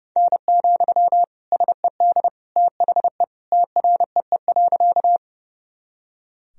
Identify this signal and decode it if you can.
Morse code, heard as a keyed tone, about 30 wpm, 700 Hz. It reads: N, SED THE TREE.
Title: Sentences from Top 300 Words and Abbreviations 30wpm